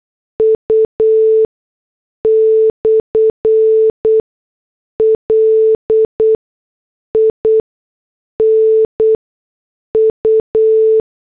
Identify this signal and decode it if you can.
Morse code, heard as a keyed tone, about 8 wpm, 440 Hz. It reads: U / L I N U
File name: morse / headlines